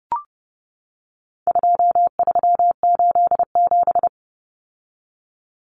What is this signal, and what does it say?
Morse code, heard as a keyed tone, about 30 wpm, 700 Hz. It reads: 2387
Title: Common Cut Numbers - 4-Digit 30wpm